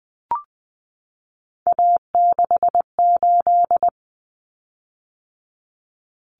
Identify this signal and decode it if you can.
Morse code, heard as a keyed tone, about 20 wpm, 700 Hz. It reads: A68
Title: Common Cut Numbers - 3-Digit 20wpm